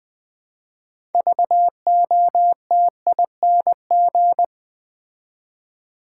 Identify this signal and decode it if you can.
Morse code, heard as a keyed tone, about 20 wpm, 700 Hz. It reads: VOTING
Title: Build A Word - 3 Letters At A Time - 6-Letter Words - 20wpm